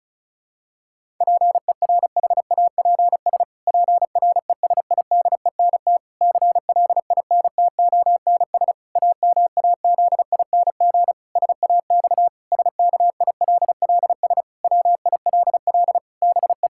Morse code, heard as a keyed tone, about 34 wpm, 700 Hz. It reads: PERHAPS PRESIDENT CLINTONS AMAZING SAX SKILLS WILL BE
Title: Pangram Challenge v2 - 35wpm